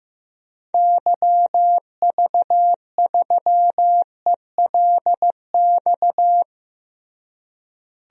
Morse code, heard as a keyed tone, about 15 wpm, 700 Hz. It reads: YV3ELX
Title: DXCC-100 Call Signs - 6 Characters - Repeated 2x - 15wpm